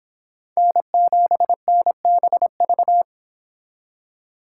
Morse code, heard as a keyed tone, about 26 wpm, 700 Hz. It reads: N7NBV
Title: Speed Racing - USA 1x3 Call Signs 22wpm